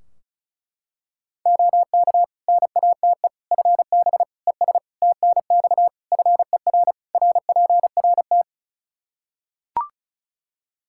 Morse code, heard as a keyed tone, about 36 wpm, 700 Hz. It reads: OK NATE FB ES TNX FER RPRT
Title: Practice QSO - Exchange 2 - Info - 35wpm